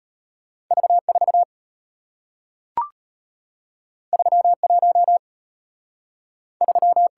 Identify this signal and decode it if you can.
Morse code, heard as a keyed tone, about 34 wpm, 700 Hz. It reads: V4 31 3